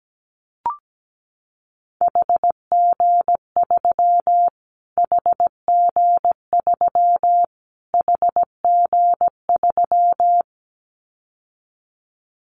Morse code, heard as a keyed tone, about 17 wpm, 700 Hz. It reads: HG3 HG3 HG3
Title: Callsign Prefixes for Top 25 DXCC - Repeated 3x - 17wpm